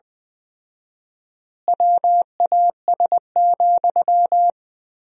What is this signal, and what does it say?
Morse code, heard as a keyed tone, about 21 wpm, 700 Hz. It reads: WAS,
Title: 4x Word-Spacing - Sentences from Top 400 Words - 20wpm